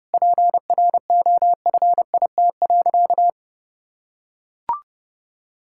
Morse code, heard as a keyed tone, about 32 wpm, 700 Hz. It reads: PROFIT.
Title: Sentences from Top 1600 Words 30wpm